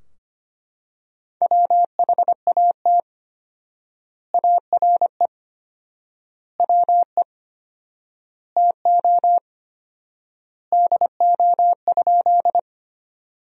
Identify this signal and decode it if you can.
Morse code, heard as a keyed tone, about 25 wpm, 700 Hz. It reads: WHAT ARE WE TO DO?
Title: 4x Word-Spacing - Sentences from Top 100 Words - 25wpm